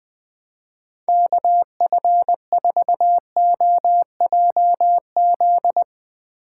Morse code, heard as a keyed tone, about 22 wpm, 700 Hz. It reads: KF4OJZ
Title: Speed Racing - USA Call Signs 20wpm